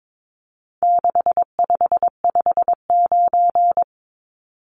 Morse code, heard as a keyed tone, about 22 wpm, 700 Hz. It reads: 6559
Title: Numbers - 4-Digit 22wpm